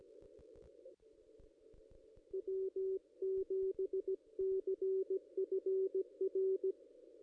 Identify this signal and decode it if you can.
Morse code, heard as a keyed tone, about 17 wpm, 380 Hz. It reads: W7CFR
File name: contests / ss-cw-k5tr-op-n5rz -1900